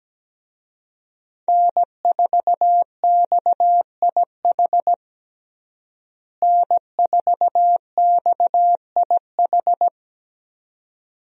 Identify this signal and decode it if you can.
Morse code, heard as a keyed tone, about 17 wpm, 700 Hz. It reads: N4XIH N4XIH
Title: USA Call Signs - 5 Characters - Repeated 2x - 17wpm